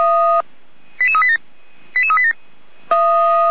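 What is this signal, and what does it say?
Five-tone selective call (ZVEI), heard as quick runs of five tones, tones of about 80 ms. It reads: 88387 88387
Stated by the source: ZVEI five-tone selective call, code 88241, received by SDR.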